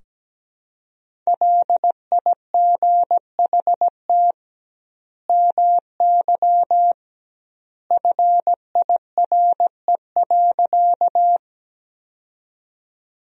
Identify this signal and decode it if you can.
Morse code, heard as a keyed tone, about 17 wpm, 700 Hz. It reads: LIGHT MY FIRE.
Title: 2x Word-Spacing - Sentences from Top 400 Words - 17wpm